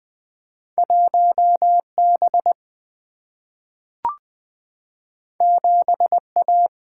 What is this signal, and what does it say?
Morse code, heard as a keyed tone, about 20 wpm, 700 Hz. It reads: 1B 7A
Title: Single-Letter Single-Number 20wpm